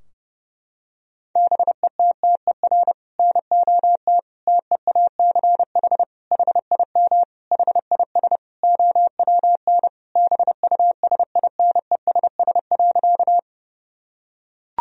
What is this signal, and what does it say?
Morse code, heard as a keyed tone, about 30 wpm, 700 Hz. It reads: BETTER NOT TEACH HIM HIS OWN BUSINESS.
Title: Sentences from Top 1000 Words - Review Part 2 - 30wpm